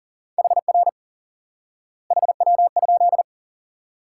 Morse code, heard as a keyed tone, about 40 wpm, 700 Hz. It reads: HR HW?
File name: IQE Course 3 - Lesson 89 - YAGI - Two Elements - 40wpm